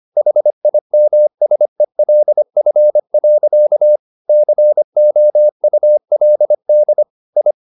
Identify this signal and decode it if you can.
Morse code, heard as a keyed tone, about 25 wpm, 600 Hz. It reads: HIMSELF. COULD I